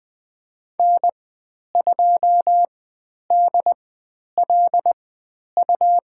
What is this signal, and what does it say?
Morse code, heard as a keyed tone, about 20 wpm, spaced as for 10 wpm, 700 Hz. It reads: N2DLU
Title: 102 Lesson 17g - Letter u - Call Signs 20wpm